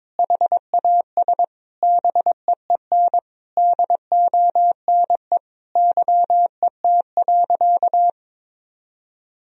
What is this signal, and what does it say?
Morse code, heard as a keyed tone, about 22 wpm, 700 Hz. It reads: HAS BEEN DONE YET.
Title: Sentences from Top 1000 Words - Review Part 2 - 22wpm